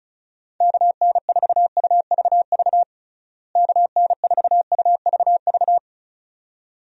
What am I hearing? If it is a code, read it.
Morse code, heard as a keyed tone, about 36 wpm, 700 Hz. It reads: KN4UVV KN4UVV
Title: USA Call Signs - 6 Characters - Repeated 2x - 35wpm